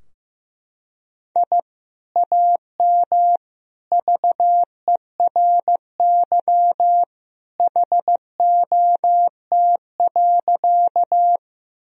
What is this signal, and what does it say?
Morse code, heard as a keyed tone, about 16 wpm, 700 Hz. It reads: I AM VERY HOT.
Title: Sentences from Top 1000 Words - Review Part 3 - 15wpm